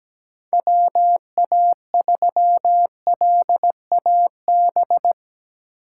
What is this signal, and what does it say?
Morse code, heard as a keyed tone, about 17 wpm, 700 Hz. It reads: WA3LAB
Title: K1USN SST QSO Practice - 17wpm